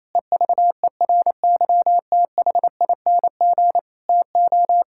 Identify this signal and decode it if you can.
Morse code, heard as a keyed tone, about 28 wpm, 700 Hz. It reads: EVERYTHING TO
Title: Sentences from Top 1000 Words - Review Part 6 - 28wpm